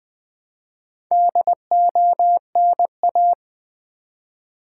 Morse code, heard as a keyed tone, about 20 wpm, 700 Hz. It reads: DONA
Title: Build A Word - 2 Letters At A Time - 8-Letter Words - 20wpm